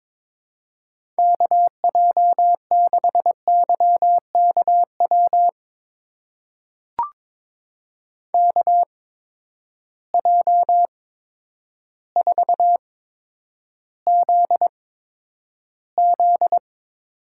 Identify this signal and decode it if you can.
Morse code, heard as a keyed tone, about 22 wpm, spaced as for 10 wpm, 700 Hz. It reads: KJ6YKW KJ4ZZ
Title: USA Call Signs - 6 Characters - 8x Character Spacing - 22wpm